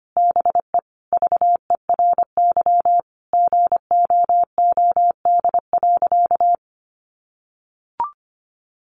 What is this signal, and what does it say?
Morse code, heard as a keyed tone, about 25 wpm, 700 Hz. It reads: BE VERY GOOD.
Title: Sentences from Top 200 Words 25wpm